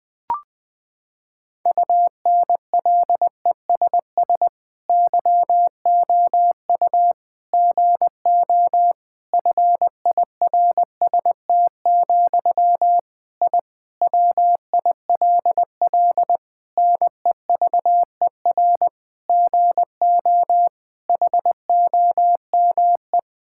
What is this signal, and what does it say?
Morse code, heard as a keyed tone, about 20 wpm, 700 Hz. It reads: UNLESS YOU GO FIRST, I WILL NEVER GO HOME
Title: Sentences from Top 1000 Words - Review Part 6 - 20wpm